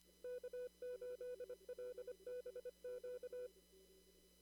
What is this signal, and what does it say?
Morse code, heard as a keyed tone, about 25 wpm, 500 Hz. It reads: K8LBQ
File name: CW / kt5j wpx cw-op-k5pi